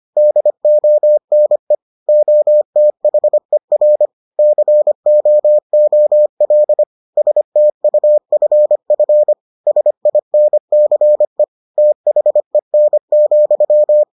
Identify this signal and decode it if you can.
Morse code, heard as a keyed tone, about 25 wpm, 600 Hz. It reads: DONE OTHER COOL STUFF SINCE THEN,